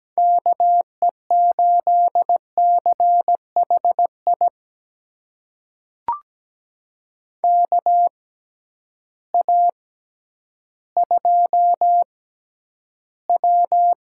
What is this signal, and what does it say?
Morse code, heard as a keyed tone, about 17 wpm, spaced as for 10 wpm, 700 Hz. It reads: KE8CHI KA2W
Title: USA Call Signs - 6 Characters - 6x Character Spacing - 17wpm